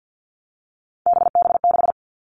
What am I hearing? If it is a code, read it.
Morse code, heard as a keyed tone, about 50 wpm, 700 Hz. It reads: BB6
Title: All Cut Numbers - 3-Digit 50wpm